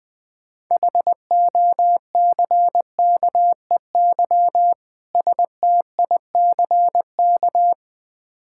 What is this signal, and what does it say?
Morse code, heard as a keyed tone, about 20 wpm, 690 Hz. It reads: HOCKEY STICK
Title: Difficult Words with Hints - 20wpm